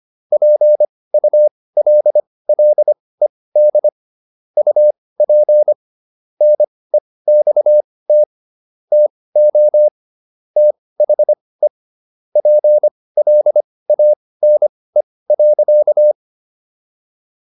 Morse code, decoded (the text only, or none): PULLED UP NEXT TO THE PLANE.